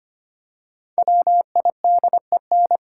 Morse code, heard as a keyed tone, about 25 wpm, 700 Hz. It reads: WIDEN
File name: 5 Letter Words - First 3 Letters Given - 25wpm